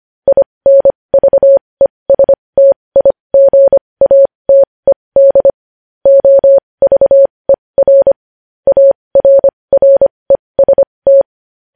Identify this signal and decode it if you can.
Morse code, heard as a keyed tone, about 25 wpm, spaced as for 20 wpm, 550 Hz. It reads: INVESTIGATED OVER ARREST
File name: podcast / News headlines-1x-25x20